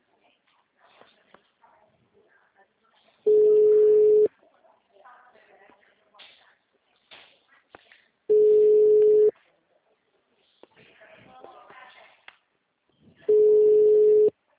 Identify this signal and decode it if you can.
Ringback tone (Europe, 425 Hz), with speech.